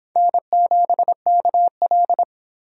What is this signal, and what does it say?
Morse code, heard as a keyed tone, about 26 wpm, 700 Hz. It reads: N7KL